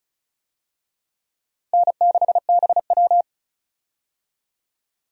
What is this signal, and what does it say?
Morse code, heard as a keyed tone, about 34 wpm, 700 Hz. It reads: N6BW
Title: USA Call Signs - 4 Characters - 35wpm